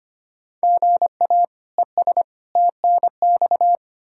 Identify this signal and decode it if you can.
Morse code, heard as a keyed tone, about 25 wpm, 700 Hz. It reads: GA ES TNX
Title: Practice QSO - Exchange 1 - Beginning - 25wpm